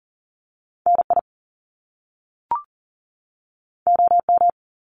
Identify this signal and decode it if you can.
Morse code, heard as a keyed tone, about 41 wpm, 700 Hz. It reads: NI OM
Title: Top Two Letter Combinations in Words 40wpm